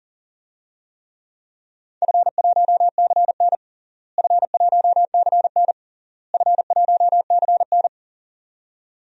Morse code, heard as a keyed tone, about 37 wpm, 700 Hz. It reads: F1CN F1CN F1CN
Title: DXCC-100 Call Signs - 4 Characters - Repeated 3x - 40wpm